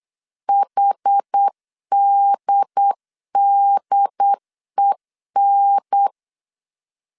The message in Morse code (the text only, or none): HDDEN